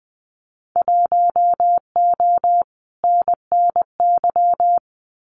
Morse code, heard as a keyed tone, about 20 wpm, 700 Hz. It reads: 1O NNY
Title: Winter Field Day Exchanges 20wpm